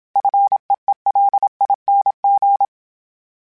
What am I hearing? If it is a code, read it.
Morse code, heard as a keyed tone, about 27 wpm, 800 Hz. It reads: FEELING
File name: Short Runs / Short Runs-mp3s / SinglePrefixes @25